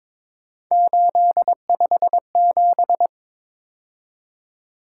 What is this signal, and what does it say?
Morse code, heard as a keyed tone, about 26 wpm, 700 Hz. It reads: 857
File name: Speed-Racing - Three Letter-Number Combinations - 22wpm